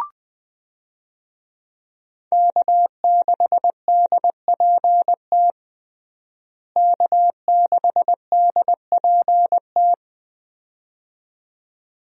Morse code, heard as a keyed tone, about 20 wpm, 700 Hz. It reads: K6DPT K6DPT
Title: USA Call Signs - 5 Characters - Repeated 2x - 20wpm